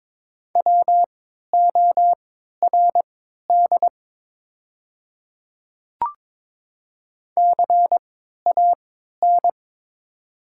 Morse code, heard as a keyed tone, about 22 wpm, spaced as for 13 wpm, 700 Hz. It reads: WORD CAN
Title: 3x Character Spacing - Top 200 Words - 22wpm